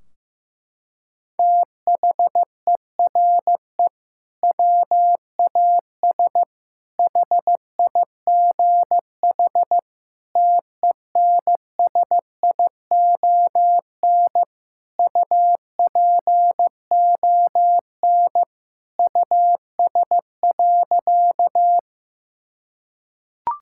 Morse code, heard as a keyed tone, about 16 wpm, 700 Hz. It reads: THERE WAS HIGH TENSION UPON US.
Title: Sentences from Top 1600 Words 15wpm